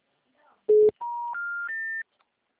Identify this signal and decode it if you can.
Special information tone.